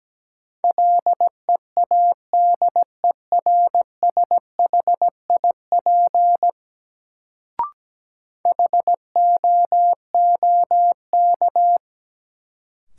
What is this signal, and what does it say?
Morse code, heard as a keyed tone, about 17 wpm, 700 Hz. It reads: LEADERSHIP HOOK
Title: Sentences from Top 1600 Words - New Word Drill Alt - 17wpm